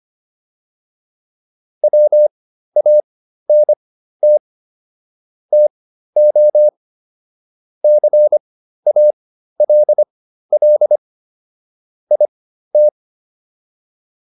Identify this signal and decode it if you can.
Morse code, heard as a keyed tone, about 25 wpm, spaced as for 13 wpm, 600 Hz. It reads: WANT TO CALL IT